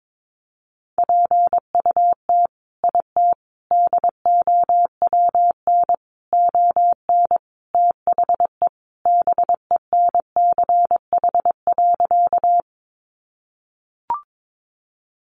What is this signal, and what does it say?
Morse code, heard as a keyed tone, about 22 wpm, 700 Hz. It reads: PUT IT DOWN ON THE BENCH.
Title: Sentences from Top 1400 Words 22wpm